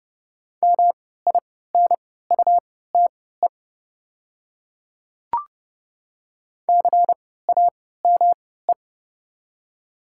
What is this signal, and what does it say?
Morse code, heard as a keyed tone, about 29 wpm, spaced as for 17 wpm, 700 Hz. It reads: MINUTE CAME
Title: 3x Character Spacing - Top 500 Words - 30wpm